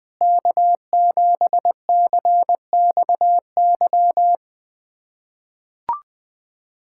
Morse code, heard as a keyed tone, about 20 wpm, 700 Hz. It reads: K7CXY
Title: USA Callsign DE Callsign - 5 Characters - 20wpm